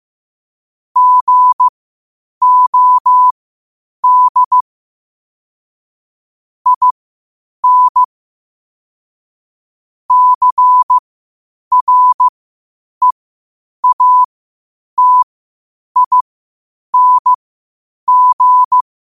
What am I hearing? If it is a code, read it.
Morse code, heard as a keyed tone, about 15 wpm, spaced as for 8 wpm, 1000 Hz. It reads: GOD IN CREATING
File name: qotdfiles / 8wpm